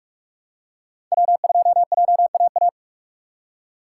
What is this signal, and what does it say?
Morse code, heard as a keyed tone, about 44 wpm, 700 Hz. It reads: W2JAA
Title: USA Call Signs - 5 Characters - 2x Character Spacing - 45wpm